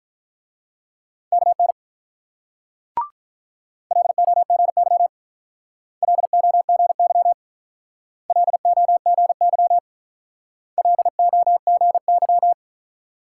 Morse code, heard as a keyed tone, about 46 wpm, 690 Hz. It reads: KN LOGY LOGY LOGY LOGY